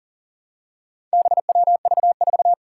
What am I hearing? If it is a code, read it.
Morse code, heard as a keyed tone, about 40 wpm, 700 Hz. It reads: BWV4
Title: Four Challenging Letter-Number Combinations 40wpm